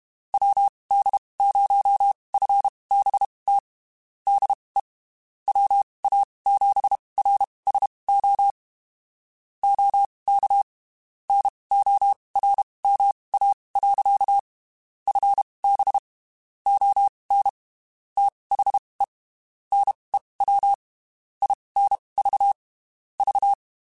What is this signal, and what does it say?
Morse code, heard as a keyed tone, about 32 wpm, spaced as for 23 wpm, 800 Hz. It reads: WD0FBT DE WA7RSO OK NORMA. FB ON THE NEW INV V
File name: (spaced) / QSOs Spaced-mp3s / WD0FBT 1 @30 sp